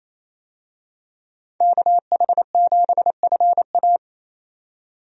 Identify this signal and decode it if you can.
Morse code, heard as a keyed tone, about 28 wpm, 700 Hz. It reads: KH7FA
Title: USA Call Signs - 5 Characters - 28wpm